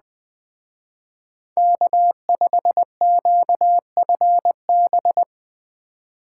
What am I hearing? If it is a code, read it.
Morse code, heard as a keyed tone, about 19 wpm, 700 Hz. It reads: K5QFB
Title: Speed Racing - USA Call Signs 17wpm